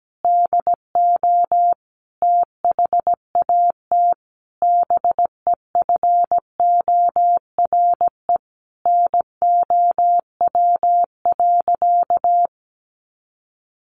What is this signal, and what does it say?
Morse code, heard as a keyed tone, about 17 wpm, 700 Hz. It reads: DO THAT BEFORE NOW.
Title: Sentences from Top 1300 Words 17wpm